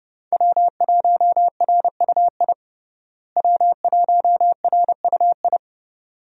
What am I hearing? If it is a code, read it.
Morse code, heard as a keyed tone, about 30 wpm, 700 Hz. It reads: W1RUI W1RUI
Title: USA Call Signs - 5 Characters - Repeated 2x - 30wpm